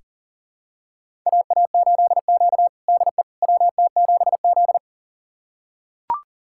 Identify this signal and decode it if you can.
Morse code, heard as a keyed tone, about 40 wpm, 700 Hz. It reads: AA8Q DE WT7Z
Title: USA Callsign DE Callsign - 4 Characters - 40wpm